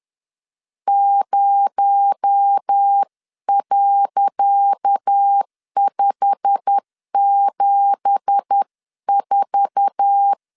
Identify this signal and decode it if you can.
Morse code, heard as a keyed tone, about 10 wpm, 800 Hz. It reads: 0.574
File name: LEZIONI / 10 SIGNS / 10C DOT 10wpm 2940